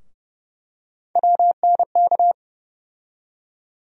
Morse code, heard as a keyed tone, about 32 wpm, 700 Hz. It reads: WNK
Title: USA Call Sign Postfixes - 3 Letters 30wpm